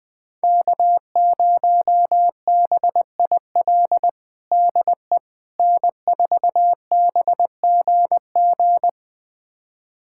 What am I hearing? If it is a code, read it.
Morse code, heard as a keyed tone, about 20 wpm, 700 Hz. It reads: K0BIL DE N4BGG
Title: USA Callsign DE Callsign - 5 Characters - 20wpm